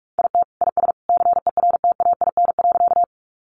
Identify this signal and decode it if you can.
Morse code, heard as a keyed tone, about 44 wpm, 700 Hz. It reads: IT IS CERTAIN.
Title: Sentences from Top 500 Words 45wpm